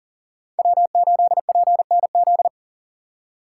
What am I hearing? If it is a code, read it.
Morse code, heard as a keyed tone, about 40 wpm, 700 Hz. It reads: W8PNZ
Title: USA Call Signs - 5 Characters - 6x Character Spacing - 40wpm